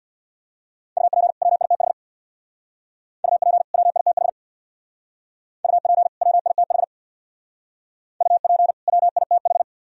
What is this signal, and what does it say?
Morse code, heard as a keyed tone, about 61 wpm, 700 Hz. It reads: UP WITH UP WITH UP WITH UP WITH
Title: Speed-Racing - Sets of 2 Words 50wpm